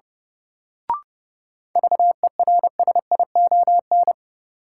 Morse code, heard as a keyed tone, about 30 wpm, 700 Hz. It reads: VERSION
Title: Sentences from Top 1400 Words - New Word Drill - 30wpm